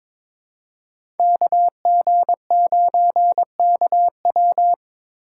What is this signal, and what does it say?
Morse code, heard as a keyed tone, about 21 wpm, 700 Hz. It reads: KG9KW
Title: USA Call Signs - 5 Characters - 4x Character Spacing - 22wpm